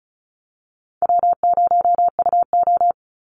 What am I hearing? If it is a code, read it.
Morse code, heard as a keyed tone, about 34 wpm, 700 Hz. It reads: W0UO
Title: Easy Call Signs Repeated 2x - 35wpm